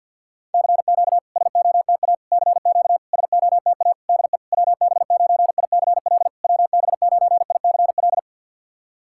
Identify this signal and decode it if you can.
Morse code, heard as a keyed tone, about 50 wpm, 700 Hz. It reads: CQ SOTA CQ SOTA DE WB9ICL WB9ICL
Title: SOTA QSO Practice 50wpm